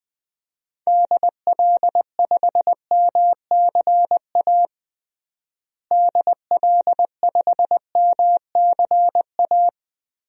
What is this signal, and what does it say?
Morse code, heard as a keyed tone, about 21 wpm, 700 Hz. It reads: DL5MCA DL5MCA
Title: DXCC-100 Call Signs - 6 Characters - Repeated 2x - 20wpm